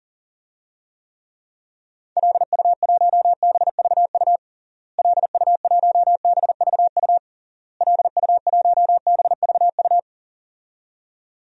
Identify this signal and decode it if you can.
Morse code, heard as a keyed tone, about 40 wpm, 700 Hz. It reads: LU1BVU LU1BVU LU1BVU
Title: DXCC-100 Call Signs - 6 Characters - Repeated 3x - 40wpm